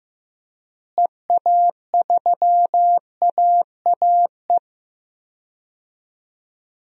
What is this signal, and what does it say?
Morse code, heard as a keyed tone, about 15 wpm, 700 Hz. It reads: EA3AAE
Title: DXCC-100 Call Signs - 6 Characters - Repeated 3x - 15wpm